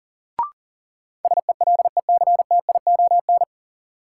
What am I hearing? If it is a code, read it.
Morse code, heard as a keyed tone, about 37 wpm, 700 Hz. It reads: SELECTION
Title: Sentences from Top 1400 Words - New Word Drill Alt - 40wpm